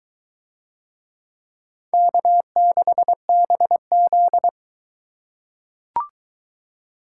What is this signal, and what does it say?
Morse code, heard as a keyed tone, about 22 wpm, 700 Hz. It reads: K6BZ